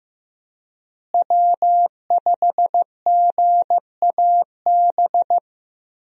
Morse code, heard as a keyed tone, about 15 wpm, 700 Hz. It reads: W5GAB
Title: USA Call Signs - 5 Characters - Repeated 2x - 15wpm